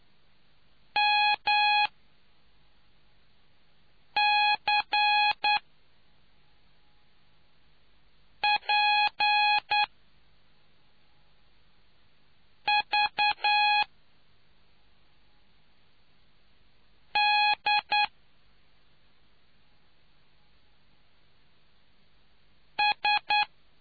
Morse code, heard as a keyed tone, about 9 wpm, spaced as for 3 wpm, 830 Hz. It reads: MCPV D S